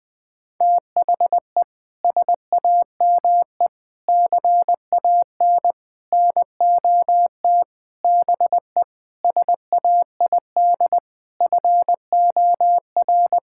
Morse code, heard as a keyed tone, about 21 wpm, 700 Hz. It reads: THE SAME CAN NOT BE SAID FOR
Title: Sentences with 8-Word Phrases - Spoken Beginning - 20wpm